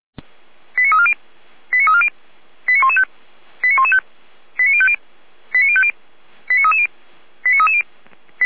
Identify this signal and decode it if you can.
Five-tone selective call (ZVEI), heard as quick runs of five tones, tones of about 70 ms. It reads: 89340 89340 89115 89115 89050 89050 89330 89330